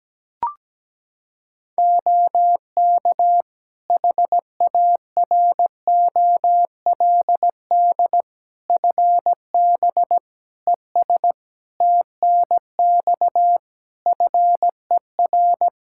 Morse code, heard as a keyed tone, about 17 wpm, 700 Hz. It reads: OK HAROLD FB ES TNX FER
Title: Practice QSO - Exchange 2 - Info - 17wpm